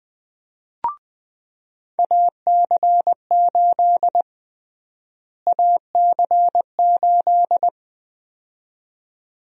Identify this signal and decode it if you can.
Morse code, heard as a keyed tone, about 20 wpm, 700 Hz. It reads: AC8 AC8
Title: USA Call Sign Prefixes - Repeated 2x - 20wpm